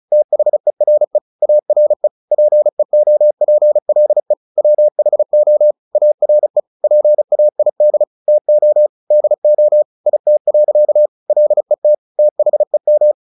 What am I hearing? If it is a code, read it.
Morse code, heard as a keyed tone, about 34 wpm, 600 Hz. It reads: THERE ARE PEOPLE WHO ARE PAID TO DO IT. LET THEM
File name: the old man and the sea 2 35wpm